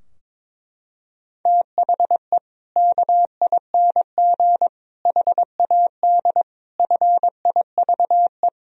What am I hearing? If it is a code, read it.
Morse code, heard as a keyed tone, about 22 wpm, 700 Hz. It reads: THE KING HAD FIVE
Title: Sentences from Top 1000 Words - Review Part 2 - 22wpm